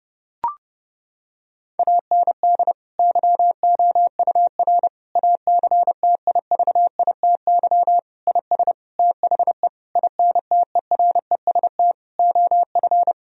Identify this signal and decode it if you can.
Morse code, heard as a keyed tone, about 30 wpm, 700 Hz. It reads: AND YOUR ACTIVITY IS THE INTEREST OF
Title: Sentences from Top 1000 Words - Review Part 4 - 30wpm